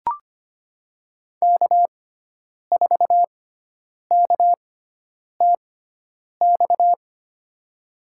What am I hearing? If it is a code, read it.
Morse code, heard as a keyed tone, about 25 wpm, spaced as for 9 wpm, 700 Hz. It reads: K4KTX
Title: USA Call Signs - 5 Characters - 6x Character Spacing - 25wpm